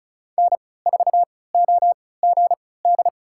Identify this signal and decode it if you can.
Morse code, heard as a keyed tone, about 34 wpm, spaced as for 20 wpm, 700 Hz. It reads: N4OGD
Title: USA Call Signs - 5 Characters - 3x Character Spacing - 35wpm